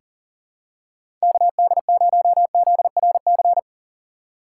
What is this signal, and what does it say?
Morse code, heard as a keyed tone, about 39 wpm, 700 Hz. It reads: KD0ZRC
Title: USA Call Signs - 6 Characters - 6x Character Spacing - 40wpm